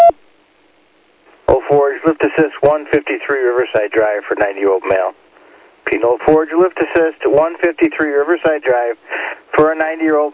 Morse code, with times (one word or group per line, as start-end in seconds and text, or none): none